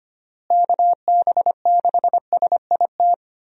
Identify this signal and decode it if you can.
Morse code, heard as a keyed tone, about 25 wpm, 700 Hz. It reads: KB6SIT
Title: USA Call Signs - 6 Characters - Repeated 3x - 25wpm